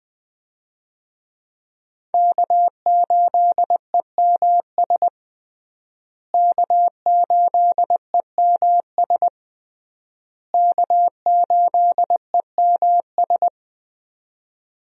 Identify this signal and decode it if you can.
Morse code, heard as a keyed tone, about 20 wpm, 700 Hz. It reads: K8EMS K8EMS K8EMS
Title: USA Call Signs - 5 Characters - Repeated 3x - 20wpm